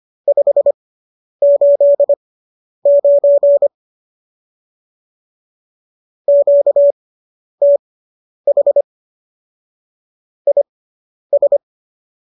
Morse code, heard as a keyed tone, about 25 wpm, spaced as for 10 wpm, 570 Hz. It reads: 589 QTH IS
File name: qso005 10